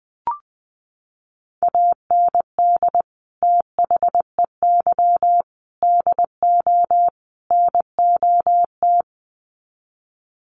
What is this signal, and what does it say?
Morse code, heard as a keyed tone, about 20 wpm, 700 Hz. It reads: AND THEY DO NOT
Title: Sets of 4 Words 20wpm